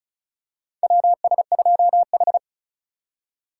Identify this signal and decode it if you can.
Morse code, heard as a keyed tone, about 36 wpm, 700 Hz. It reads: WS2H
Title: USA Call Signs - 4 Characters - 2x Character Spacing - 35wpm